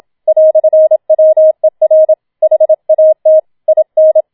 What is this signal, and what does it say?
Morse code, heard as a keyed tone, about 27 wpm, 630 Hz. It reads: "WER HAT IN